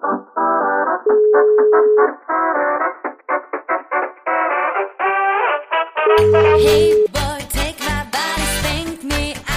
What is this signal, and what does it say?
Ringback tone (Europe, 425 Hz), with noise.